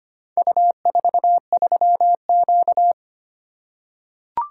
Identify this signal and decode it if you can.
Morse code, heard as a keyed tone, about 25 wpm, 700 Hz. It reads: U43Q
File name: practice / 25wpm / Four Letter-Number Combinations - 25wpm